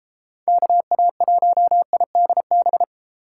Morse code, heard as a keyed tone, about 33 wpm, 700 Hz. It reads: KA1IDB